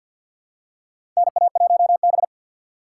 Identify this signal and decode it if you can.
Morse code, heard as a keyed tone, about 45 wpm, 700 Hz. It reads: NA1B
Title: USA Call Signs - 4 Characters - 3x Character Spacing - 50wpm